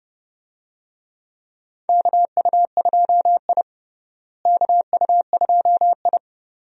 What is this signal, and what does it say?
Morse code, heard as a keyed tone, about 29 wpm, 700 Hz. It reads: KU2I KU2I
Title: USA Call Signs - 4 Characters - Repeated 2x - 30wpm